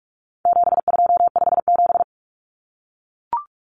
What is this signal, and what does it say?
Morse code, heard as a keyed tone, about 44 wpm, 700 Hz. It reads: 7257
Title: Numbers - 4-Digit 45wpm